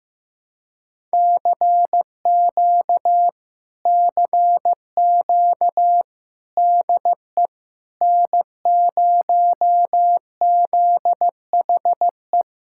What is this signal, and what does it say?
Morse code, heard as a keyed tone, about 15 wpm, 700 Hz. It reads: CQ CQ DE N0ZHE
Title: Speed-Racing Common QSO Phrases Encore - 15wpm